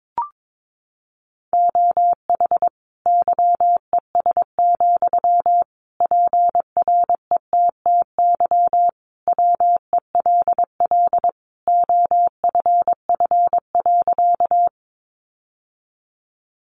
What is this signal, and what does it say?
Morse code, heard as a keyed tone, about 22 wpm, 700 Hz. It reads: OH YES, PRETTY WELL OFF.
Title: Sentences from Top 1000 Words 22wpm